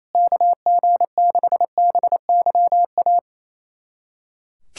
Morse code, heard as a keyed tone, about 28 wpm, 700 Hz. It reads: KG6BYA
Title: USA Callsign DE Callsign 28wpm